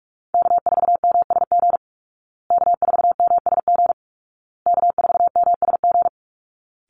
Morse code, heard as a keyed tone, about 44 wpm, 700 Hz. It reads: K4MSG K4MSG K4MSG
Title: Call Signs Repeated 2x - 45wpm